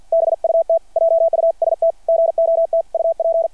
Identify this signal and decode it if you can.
Morse code, heard as a keyed tone, about 50 wpm, 660 Hz. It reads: BUT JUST GOT UP
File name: archived / CRSnet